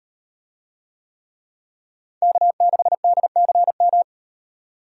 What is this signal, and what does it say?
Morse code, heard as a keyed tone, about 36 wpm, 700 Hz. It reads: K6DCM